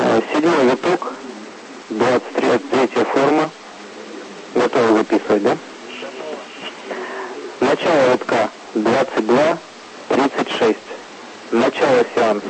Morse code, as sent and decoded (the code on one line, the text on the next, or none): none